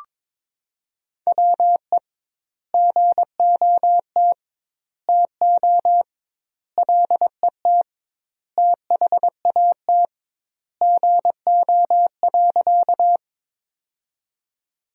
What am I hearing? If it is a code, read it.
Morse code, heard as a keyed tone, about 22 wpm, 700 Hz. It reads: WE GOT TO LET THAT GO.
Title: 2x Word-Spacing - Sentences from Top 300 Words - 22wpm